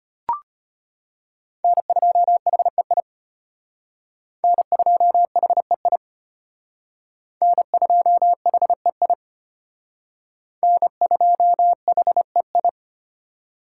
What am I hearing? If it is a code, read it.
Morse code, heard as a keyed tone, about 32 wpm, 700 Hz. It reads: N2HEI N2HEI N2HEI N2HEI